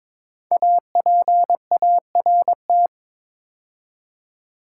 Morse code, heard as a keyed tone, about 23 wpm, 700 Hz. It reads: APART
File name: 5 Letter Words - First 3 Letters Given - 22wpm